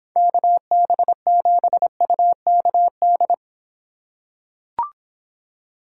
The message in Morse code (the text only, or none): KB7UKD